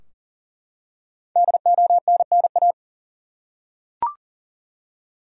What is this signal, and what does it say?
Morse code, heard as a keyed tone, about 40 wpm, 700 Hz. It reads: DONNA
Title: USA Common Names - 40wpm